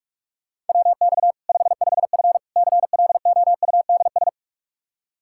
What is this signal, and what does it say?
Morse code, heard as a keyed tone, about 41 wpm, 700 Hz. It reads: WX 55F CLOUDS
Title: Practice QSO - Exchange 2 - Info - 45wpm